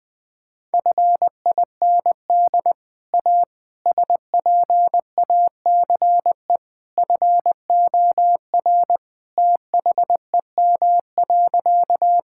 Morse code, heard as a keyed tone, about 20 wpm, 700 Hz. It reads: FIND A SPACE FOR THEM.
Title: Sentences from Top 1000 Words - Review Part 5 - 20wpm